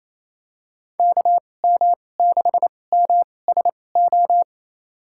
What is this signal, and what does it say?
Morse code, heard as a keyed tone, about 28 wpm, spaced as for 20 wpm, 700 Hz. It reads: KM6MSO
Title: USA Call Signs - 2x Character Spacing - 28wpm